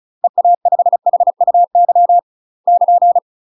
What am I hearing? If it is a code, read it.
Morse code, heard as a keyed tone, about 36 wpm, 700 Hz. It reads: EA5HUY <KN>
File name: UK MASTER QSOs TRAINING / 35WPM / UK MASTER QSOs 0007